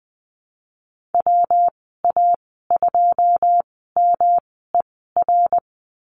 Morse code, heard as a keyed tone, about 20 wpm, spaced as for 14 wpm, 700 Hz. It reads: WA2MER
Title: 128 Lesson 21g - Number 6 - Call Signs 20wpm